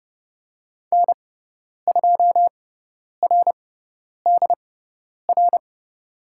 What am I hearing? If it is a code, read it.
Morse code, heard as a keyed tone, about 30 wpm, spaced as for 10 wpm, 700 Hz. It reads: N2RDR
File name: Lesson 17g - Letter u - Call Signs 30wpm